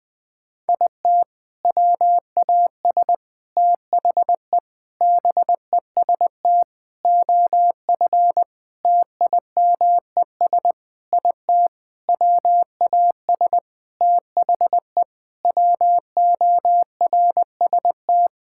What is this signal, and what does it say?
Morse code, heard as a keyed tone, about 20 wpm, 700 Hz. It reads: IT WAS THE BEST OF TIMES IT WAS THE WORST
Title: Sentences with 8-Word Phrases - 20wpm